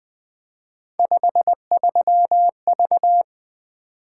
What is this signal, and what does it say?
Morse code, heard as a keyed tone, about 20 wpm, 700 Hz. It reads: 53V
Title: Three Letter-Number Combinations - 20wpm